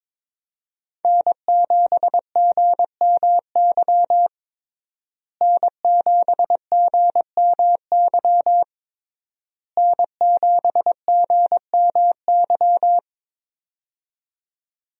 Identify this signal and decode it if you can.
Morse code, heard as a keyed tone, about 22 wpm, 700 Hz. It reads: N7GMY N7GMY N7GMY
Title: USA Call Signs - 5 Characters - Repeated 3x - 22wpm